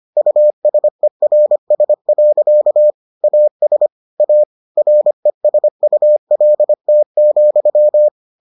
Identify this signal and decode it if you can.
Morse code, heard as a keyed tone, about 25 wpm, 600 Hz. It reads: USERS. AS A RESULT,